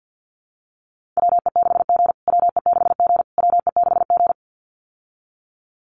Morse code, heard as a keyed tone, about 50 wpm, 700 Hz. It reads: WE6G WE6G WE6G
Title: USA Call Signs Repeated 3x 50wpm